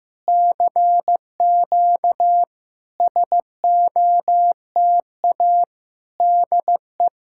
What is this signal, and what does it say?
Morse code, heard as a keyed tone, about 15 wpm, 700 Hz. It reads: CQ SOTA DE
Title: SOTA QSO Practice 15wpm